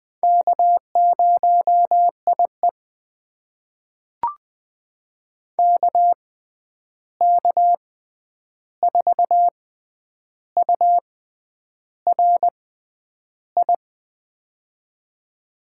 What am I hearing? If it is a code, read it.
Morse code, heard as a keyed tone, about 20 wpm, spaced as for 9 wpm, 700 Hz. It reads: K0IE KK4URI
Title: USA Call Signs - 6x Character Spacing - 20wpm